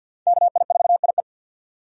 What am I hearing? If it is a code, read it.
Morse code, heard as a keyed tone, about 47 wpm, 700 Hz. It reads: KI4IE
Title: USA Call Signs - 5 Characters - 3x Character Spacing - 50wpm